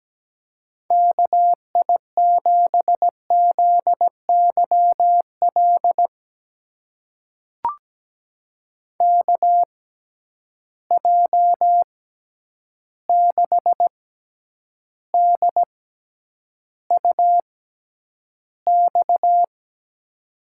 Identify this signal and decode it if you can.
Morse code, heard as a keyed tone, about 17 wpm, spaced as for 9 wpm, 700 Hz. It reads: KI7ZYL KJ6DUX
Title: USA Call Signs - 6x Character Spacing - 17wpm